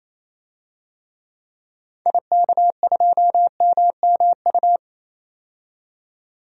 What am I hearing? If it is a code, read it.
Morse code, heard as a keyed tone, about 28 wpm, 700 Hz. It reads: IK2MMU
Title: DXCC-100 Call Signs - 6 Characters - 28wpm